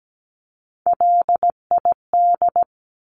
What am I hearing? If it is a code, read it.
Morse code, heard as a keyed tone, about 17 wpm, 700 Hz. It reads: LID
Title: Abbreviations 17wpm